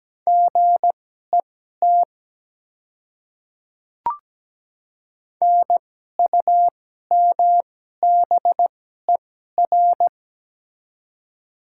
Morse code, heard as a keyed tone, about 17 wpm, spaced as for 12 wpm, 700 Hz. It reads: GET NUMBER
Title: 2x Character Spacing - Top 300 Words - 17wpm